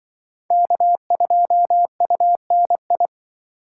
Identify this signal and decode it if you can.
Morse code, heard as a keyed tone, about 23 wpm, 700 Hz. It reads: K2UNI
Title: Speed Racing - USA 1x3 Call Signs 20wpm